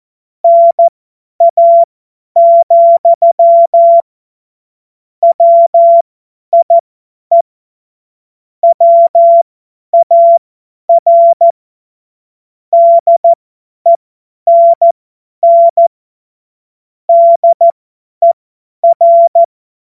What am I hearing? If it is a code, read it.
Morse code, heard as a keyed tone, about 14 wpm, spaced as for 10 wpm, 680 Hz. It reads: NA, WIE WAR DENN DER
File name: Morsekurs / Witze / Witz 21. März 2023 50 F70